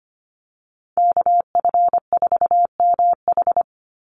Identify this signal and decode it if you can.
Morse code, heard as a keyed tone, about 25 wpm, 700 Hz. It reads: KF4MH
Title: NA SOTA Chaser Callsigns 25wpm